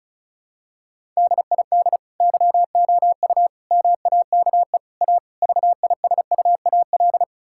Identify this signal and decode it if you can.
Morse code, heard as a keyed tone, about 36 wpm, 700 Hz. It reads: DID YOU MAKE A VISUAL
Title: Sentences from Top 1000 Words - Review Part 6 - 35wpm